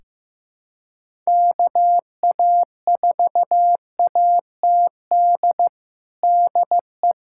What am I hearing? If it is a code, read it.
Morse code, heard as a keyed tone, about 15 wpm, 700 Hz. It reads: KA4ATD DE
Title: USA Callsign DE Callsign - 6 Characters - 15wpm